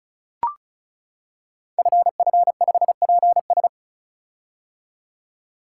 Morse code, heard as a keyed tone, about 36 wpm, 700 Hz. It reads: FF5PS
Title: Code Groups - 5 Letter:Numbers - 35wpm